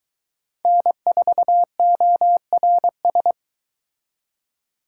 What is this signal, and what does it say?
Morse code, heard as a keyed tone, about 21 wpm, 700 Hz. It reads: N4ORS